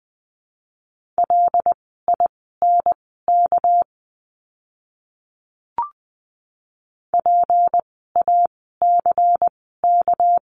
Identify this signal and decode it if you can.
Morse code, heard as a keyed tone, about 20 wpm, spaced as for 14 wpm, 700 Hz. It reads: LINK PACK
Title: 199 Lesson 32f - Letter k - Words 20wpm